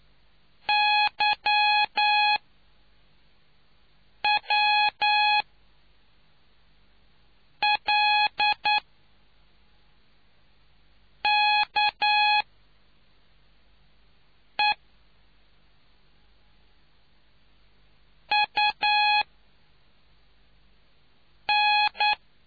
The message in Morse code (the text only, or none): Y W L K E U N